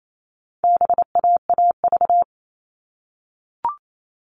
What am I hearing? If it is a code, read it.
Morse code, heard as a keyed tone, about 28 wpm, 700 Hz. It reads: BAAV
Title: All Cut Numbers - 4-Digit 28wpm